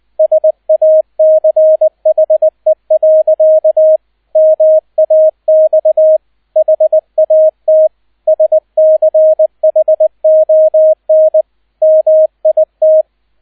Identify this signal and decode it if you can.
Morse code, heard as a keyed tone, about 20 wpm, 630 Hz, with radio noise. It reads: SACHE. MAX HAT SCHON MIT
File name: 002 Zweiter Streich 20wpm